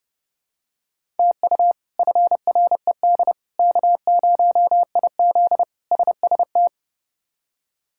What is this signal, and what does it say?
Morse code, heard as a keyed tone, about 30 wpm, 700 Hz. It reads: TU FRED K0IZ SST
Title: K1USN SST QSO Practice - 30wpm